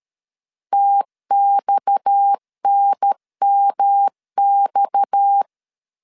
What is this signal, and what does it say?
Morse code, heard as a keyed tone, about 13 wpm, 800 Hz. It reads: TXNMX
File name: LEZIONI / 6 PKX0 / 6 PKX 12wpm 2024